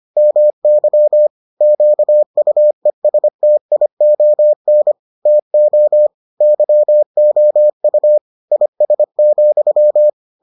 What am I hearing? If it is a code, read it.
Morse code, heard as a keyed tone, about 25 wpm, 600 Hz. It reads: MY QUESTION TO YOU IS,